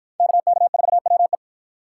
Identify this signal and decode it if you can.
Morse code, heard as a keyed tone, about 46 wpm, 700 Hz. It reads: KK4WE